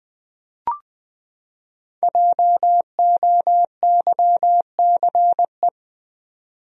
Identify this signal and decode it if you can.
Morse code, heard as a keyed tone, about 20 wpm, 700 Hz. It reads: JOYCE
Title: USA Common Names - 20wpm